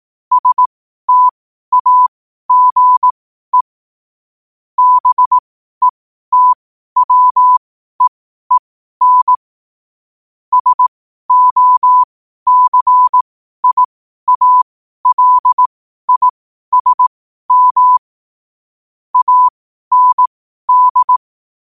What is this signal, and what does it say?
Morse code, heard as a keyed tone, about 18 wpm, spaced as for 12 wpm, 1000 Hz. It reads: STAGE BETWEEN SOCIALISM AND
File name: qotdfiles / 12wpm